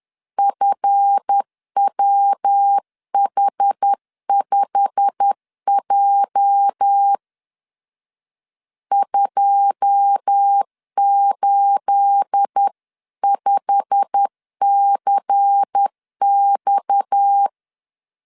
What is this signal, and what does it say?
Morse code, heard as a keyed tone, about 10 wpm, 800 Hz. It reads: FWH5J 285CX